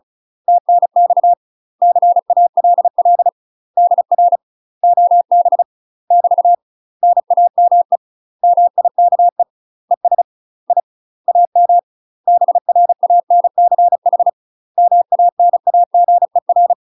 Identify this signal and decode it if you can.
Morse code, heard as a keyed tone, about 34 wpm, 700 Hz. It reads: TNX CALL DR OB <BT> NAME MIKE ES I AM BRANCH MANAGER
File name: UK MASTER QSOs TRAINING / 35WPM / UK MASTER QSOs 0015